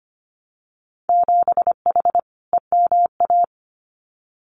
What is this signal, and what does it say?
Morse code, heard as a keyed tone, about 25 wpm, 700 Hz. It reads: 7H EMA
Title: Winter Field Day Exchanges Encore 25wpm